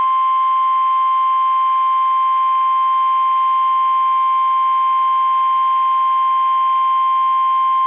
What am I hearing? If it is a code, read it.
Emergency Alert System (SAME) tones (the 1050 Hz attention tone). It is heard from 0.9 to 7.3 s.
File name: eas / KUSO / KUSO